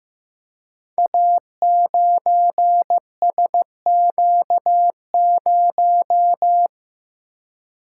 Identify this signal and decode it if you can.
Morse code, heard as a keyed tone, about 15 wpm, 700 Hz. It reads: A9SQ0
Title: Code Groups - 5 Letter:Numbers - 15wpm